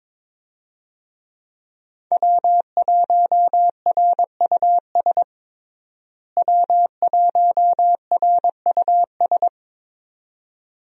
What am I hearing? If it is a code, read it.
Morse code, heard as a keyed tone, about 21 wpm, 700 Hz. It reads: W1RUS W1RUS
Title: USA Call Signs - 5 Characters - Repeated 2x - 22wpm